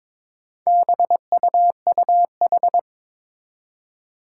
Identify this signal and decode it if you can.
Morse code, heard as a keyed tone, about 22 wpm, 700 Hz. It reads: BUUH
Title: Four Challenging Letter-Number Combinations 22wpm